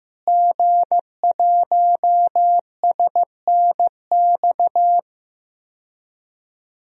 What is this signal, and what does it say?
Morse code, heard as a keyed tone, about 15 wpm, 700 Hz. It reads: G1SNX
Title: DXCC-100 Call Signs - 5 Characters - Repeated 3x - 15wpm